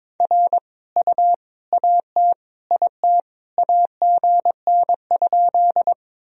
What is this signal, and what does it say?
Morse code, heard as a keyed tone, about 22 wpm, 700 Hz. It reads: R U AT IT AGN?
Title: Sentences from Top 200 Words and Abbreviations 22wpm